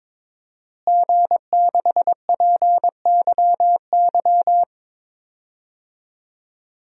Morse code, heard as a keyed tone, about 22 wpm, 700 Hz. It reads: G6PYY
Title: DXCC-100 Call Signs - 5 Characters - Repeated 3x - 22wpm